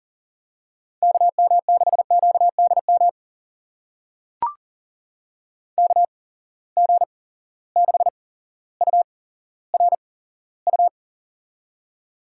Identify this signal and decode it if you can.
Morse code, heard as a keyed tone, about 40 wpm, spaced as for 17 wpm, 700 Hz. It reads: KM6QDM KG6URU
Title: USA Call Signs - 6 Characters - 8x Character Spacing - 40wpm